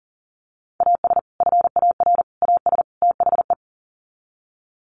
Morse code, heard as a keyed tone, about 40 wpm, 700 Hz. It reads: AS FAR AS THE
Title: Sets of 4 Words 40wpm